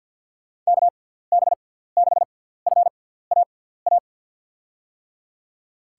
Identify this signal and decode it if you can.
Morse code, heard as a keyed tone, about 52 wpm, spaced as for 17 wpm, 700 Hz. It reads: KB6FAA
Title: USA Call Signs - 6 Characters - 6x Character Spacing - 50wpm